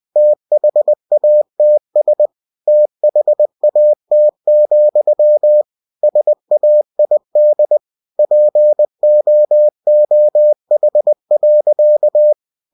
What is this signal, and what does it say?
Morse code, heard as a keyed tone, about 20 wpm, 600 Hz. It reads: THATS THAT, SAID POOH.
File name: cw-books / winnie-the-pooh / winnie-20wpm-page-0076